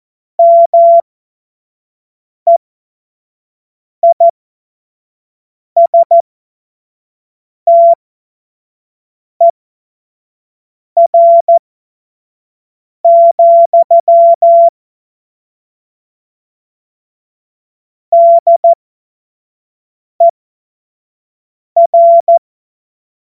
Morse code, heard as a keyed tone, about 14 wpm, spaced as for 5 wpm, 680 Hz. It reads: MEISTER, DER
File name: Morsekurs / Witze / Witz 15. November 2022 25 F70